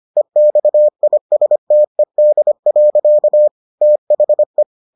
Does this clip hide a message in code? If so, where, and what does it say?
Yes: Morse code from 0.2 to 4.6 s, heard as a keyed tone, about 25 wpm, 600 Hz. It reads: EXISTED. THE